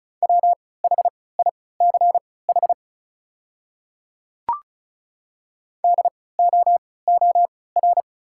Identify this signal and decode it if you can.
Morse code, heard as a keyed tone, about 34 wpm, spaced as for 20 wpm, 690 Hz. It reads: WHICH DOOR
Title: 3x Character Spacing - Top 300 Words - 35wpm